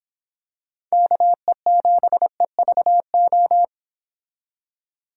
Morse code, heard as a keyed tone, about 26 wpm, 700 Hz. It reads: KE7EVO